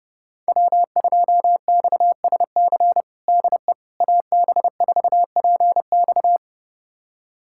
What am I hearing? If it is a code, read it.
Morse code, heard as a keyed tone, about 30 wpm, 700 Hz. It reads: W2XSC DE AB4PX
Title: USA Callsign DE Callsign - 5 Characters - 30wpm